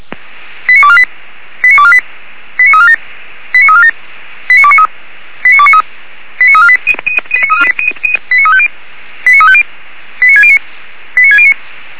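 Five-tone selective call (ZVEI), heard as quick runs of five tones, tones of about 70 ms. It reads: 89248 89248 89347 89347 89293 89293 89348 89350 89350 89609 89609
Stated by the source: ZVEI five-tone selective call 89460 received on SDR, 464 channel.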